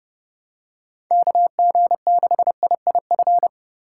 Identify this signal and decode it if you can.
Morse code, heard as a keyed tone, about 30 wpm, 700 Hz. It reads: KG6IIF
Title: USA Call Signs - 6 Characters - 4x Character Spacing - 30wpm